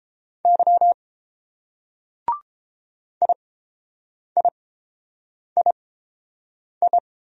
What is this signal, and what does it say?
Morse code, heard as a keyed tone, about 30 wpm, spaced as for 7 wpm, 700 Hz. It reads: Y IIII